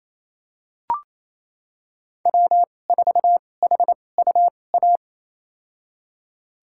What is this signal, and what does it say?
Morse code, heard as a keyed tone, about 28 wpm, spaced as for 20 wpm, 700 Hz. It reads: W4HUA
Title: USA Call Signs - 5 Characters - 2x Character Spacing - 28wpm